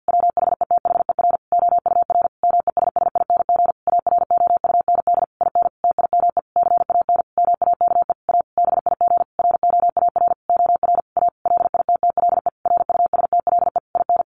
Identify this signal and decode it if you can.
Morse code, heard as a keyed tone, about 50 wpm, 700 Hz. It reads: WHETHER OUR MESSING AROUND IN TIME CAN MAKE A BIG ROAR OR A LITTLE RUSTLE IN
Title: A Sound Of Thunder 50wpm